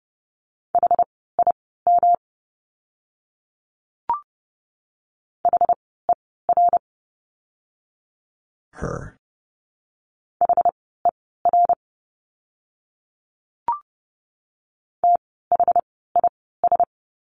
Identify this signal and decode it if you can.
Morse code, heard as a keyed tone, about 30 wpm, spaced as for 17 wpm, 700 Hz. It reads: HIM HER HER THIS
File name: course / 30:10wpm / 237 Lesson 38f - Letter q - Words 30wpm